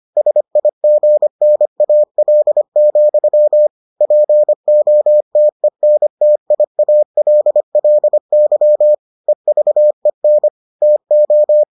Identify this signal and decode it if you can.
Morse code, heard as a keyed tone, about 25 wpm, 600 Hz. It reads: SIGNAL, POTENTIALLY EVEN TO